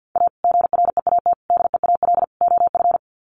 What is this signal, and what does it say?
Morse code, heard as a keyed tone, about 50 wpm, 700 Hz. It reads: A GREAT DEAL OF
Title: Sets of 4 Words 50wpm